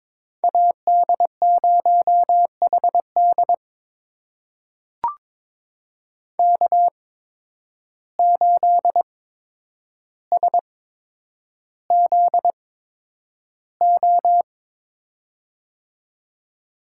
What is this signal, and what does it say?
Morse code, heard as a keyed tone, about 22 wpm, spaced as for 9 wpm, 700 Hz. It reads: AD0HD K8SZO
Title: USA Call Signs - 5 Characters - 8x Character Spacing - 22wpm